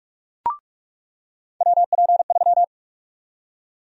Morse code, heard as a keyed tone, about 42 wpm, 700 Hz. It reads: WP3
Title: USA Call Sign Prefixes - 45wpm